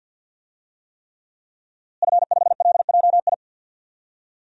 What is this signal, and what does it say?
Morse code, heard as a keyed tone, about 52 wpm, 700 Hz. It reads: F5LJI
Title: DXCC-100 Call Signs - 5 Characters - Repeated 3x - 50wpm